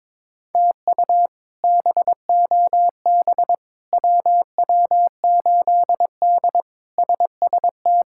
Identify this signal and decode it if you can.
Morse code, heard as a keyed tone, about 22 wpm, 700 Hz. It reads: TU BOB WW8D SST
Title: K1USN SST QSO Practice - 22wpm